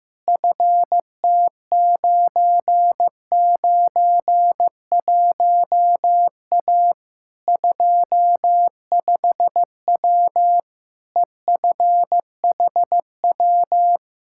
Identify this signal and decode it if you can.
Morse code, heard as a keyed tone, about 15 wpm, 700 Hz. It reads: FT991A 25W EFHW
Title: Practice QSO - Exchange 2 - Info - 15wpm